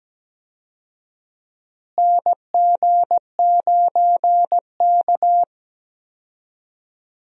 Morse code, heard as a keyed tone, about 17 wpm, 700 Hz. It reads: NG9K
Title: USA Call Signs - 4 Characters - Repeated 2x - 17wpm